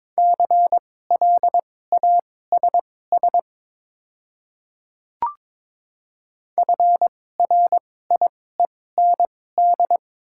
Morse code, heard as a keyed tone, about 22 wpm, spaced as for 16 wpm, 700 Hz. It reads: CLASS FRIEND
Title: 2x Character Spacing - Top 500 Words - 22wpm